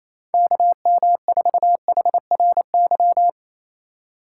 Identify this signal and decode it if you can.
Morse code, heard as a keyed tone, about 28 wpm, 700 Hz. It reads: KM4HRY
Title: USA Call Signs - 6 Characters - 2x Character Spacing - 28wpm